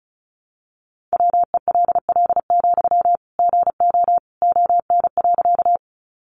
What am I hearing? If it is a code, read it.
Morse code, heard as a keyed tone, about 34 wpm, 700 Hz. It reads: WELL, GO ON.
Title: Sentences from Top 200 Words 35wpm